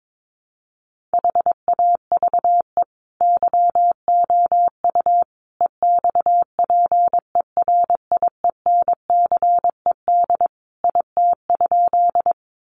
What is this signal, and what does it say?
Morse code, heard as a keyed tone, about 22 wpm, 700 Hz. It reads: HAVE YOU EXPERIENCED IT?
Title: Sentences from Top 1100 Words 22wpm